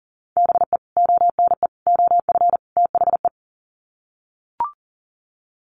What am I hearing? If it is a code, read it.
Morse code, heard as a keyed tone, about 40 wpm, 700 Hz. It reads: BE ONE OF THE
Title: Sets of 4 Words 40wpm